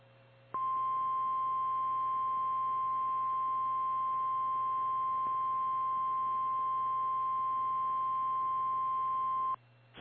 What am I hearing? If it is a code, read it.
Emergency Alert System (SAME) tones (the 1050 Hz attention tone). It is heard from 0.5 to 9.6 s.
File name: eas / KKPRKXPN / KKPRKXPN